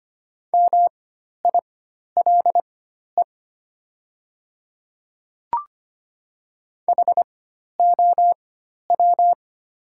Morse code, heard as a keyed tone, about 25 wpm, spaced as for 12 wpm, 700 Hz. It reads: MILE HOW
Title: 4x Character Spacing - Top 500 Words - 25wpm